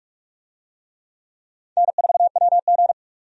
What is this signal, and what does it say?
Morse code, heard as a keyed tone, about 43 wpm, 700 Hz. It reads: N4WG
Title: USA Call Signs - 4 Characters - 45wpm